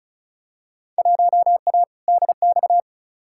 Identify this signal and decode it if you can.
Morse code, heard as a keyed tone, about 36 wpm, 700 Hz. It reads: 1A DX
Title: Field Day Exchanges Encore - 35wpm